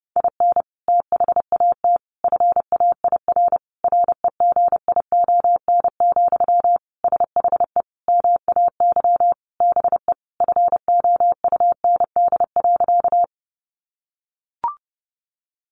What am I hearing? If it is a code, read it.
Morse code, heard as a keyed tone, about 30 wpm, 700 Hz. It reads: IN THAT FAIR REGION, SHE MAY BE FOUND.
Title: Sentences from Top 1500 Words 30wpm